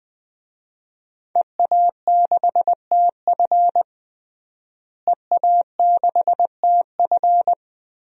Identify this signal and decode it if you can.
Morse code, heard as a keyed tone, about 20 wpm, 700 Hz. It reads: EA6TF EA6TF
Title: DXCC-100 Call Signs - 5 Characters - Repeated 3x - 20wpm